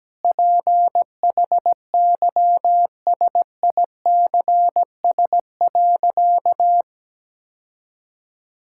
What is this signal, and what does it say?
Morse code, heard as a keyed tone, about 17 wpm, 700 Hz. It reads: PHYSICS.
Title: Sentences from Top 1000 Words - Review Part 5 - 17wpm